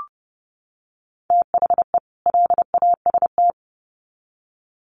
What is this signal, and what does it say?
Morse code, heard as a keyed tone, about 30 wpm, 700 Hz. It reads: THE LAST
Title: Sets of 2 Words 30wpm